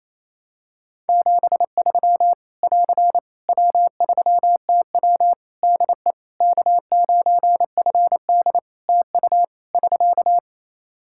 Morse code, heard as a keyed tone, about 28 wpm, 700 Hz. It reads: 73 <AR> W3TW DE K9FD TU <SK>
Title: Practice QSO - Exchange 4 - Ending - 28wpm